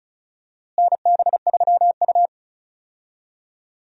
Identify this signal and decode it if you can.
Morse code, heard as a keyed tone, about 36 wpm, 700 Hz. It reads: NB3U
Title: USA Call Signs - 4 Characters - Repeated 2x - 35wpm